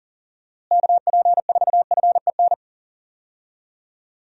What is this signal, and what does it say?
Morse code, heard as a keyed tone, about 40 wpm, 700 Hz. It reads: KP4FEN
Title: DXCC-100 Call Signs - 6 Characters - Repeated 2x - 40wpm